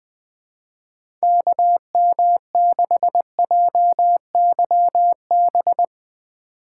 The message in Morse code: KM6JYB

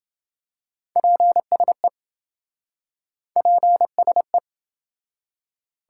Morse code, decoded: PSE PSE

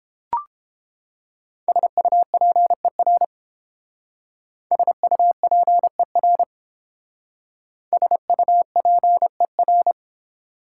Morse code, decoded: SUPER SUPER SUPER

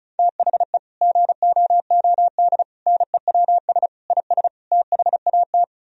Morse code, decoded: THE GOOD NEWS IS THAT

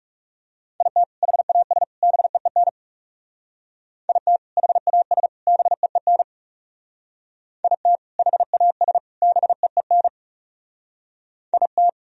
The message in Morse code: IT HAS BEEN IT HAS BEEN IT HAS BEEN IT

